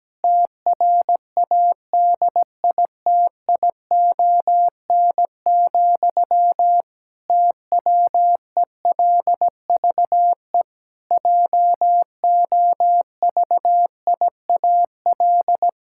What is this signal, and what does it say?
Morse code, heard as a keyed tone, about 17 wpm, 700 Hz. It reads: TRADITION, TWELVE JOVIAL
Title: Pangram Challenge v2 - 17wpm